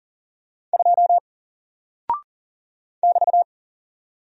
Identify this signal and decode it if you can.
Morse code, heard as a keyed tone, about 41 wpm, 700 Hz. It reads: 2 <BT>
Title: Single Character (Letter:Number:Punctuation:Prosign) 40wpm